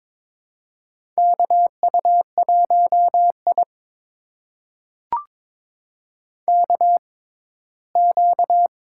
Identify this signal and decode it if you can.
Morse code, heard as a keyed tone, about 22 wpm, spaced as for 15 wpm, 700 Hz. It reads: KU1I KQ
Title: USA Call Signs - 4 Characters - 6x Character Spacing - 22wpm